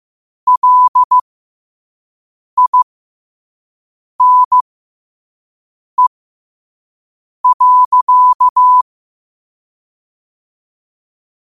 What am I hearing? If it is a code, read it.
Morse code, heard as a keyed tone, about 15 wpm, spaced as for 5 wpm, 1000 Hz. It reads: LINE.
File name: qotdfiles / 5wpm